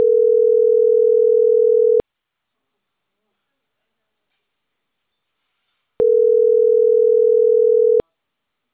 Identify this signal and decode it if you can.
Ringback tone (North America), 440 + 480 Hz.